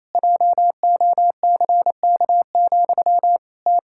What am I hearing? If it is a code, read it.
Morse code, heard as a keyed tone, about 28 wpm, 700 Hz. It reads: JOCK, T
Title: Pangram Challenge v2 - 28wpm